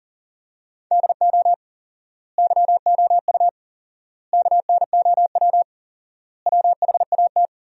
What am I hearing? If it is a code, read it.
Morse code, heard as a keyed tone, about 40 wpm, 700 Hz. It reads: DO YOU KNOW WHAT
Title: 4x Word-Spacing - Sentences from Top 100 Words - 40wpm